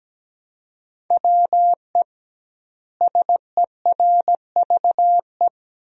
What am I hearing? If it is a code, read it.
Morse code, heard as a keyed tone, about 17 wpm, 700 Hz. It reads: WE SERVE
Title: 2x Word-Spacing - Sentences from Top 400 Words - 17wpm